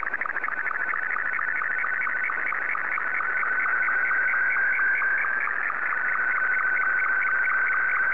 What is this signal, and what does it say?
SSTV (slow-scan television), Martin 2.
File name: sstv / M2-VK3